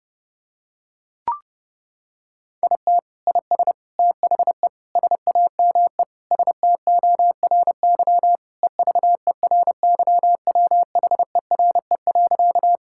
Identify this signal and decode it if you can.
Morse code, heard as a keyed tone, about 30 wpm, 700 Hz. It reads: IT IS THE SAME STORY EVERYWHERE.
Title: Sentences from Top 1600 Words 30wpm